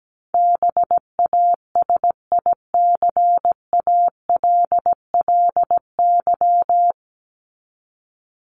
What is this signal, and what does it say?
Morse code, heard as a keyed tone, about 17 wpm, 700 Hz. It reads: BASICALLY
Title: Top 1251-1500 Words 17wpm